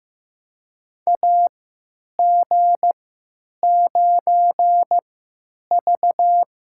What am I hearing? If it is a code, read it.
Morse code, heard as a keyed tone, about 15 wpm, spaced as for 9 wpm, 700 Hz. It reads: AG9V
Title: USA Call Signs - 4 Characters - 3x Character Spacing - 15wpm